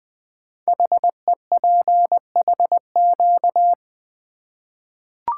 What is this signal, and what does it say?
Morse code, heard as a keyed tone, about 20 wpm, 700 Hz. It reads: HEPHQ
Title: Code Groups - 5 Letters - 20wpm